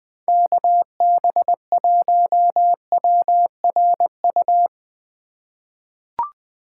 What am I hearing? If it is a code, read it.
Morse code, heard as a keyed tone, about 20 wpm, 700 Hz. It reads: KB1WRU
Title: USA Call Signs - 6 Characters - 2x Character Spacing - 20wpm